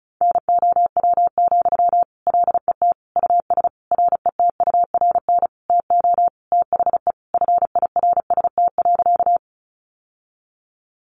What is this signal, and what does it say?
Morse code, heard as a keyed tone, about 34 wpm, 700 Hz. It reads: NOW, LET US RETURN TO THE FIRST.
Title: Sentences from Top 800 Words 35wpm